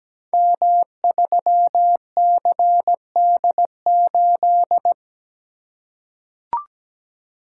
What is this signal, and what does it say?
Morse code, heard as a keyed tone, about 17 wpm, 700 Hz. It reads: M3CD8
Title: Code Groups - 5 Letter:Numbers - 17wpm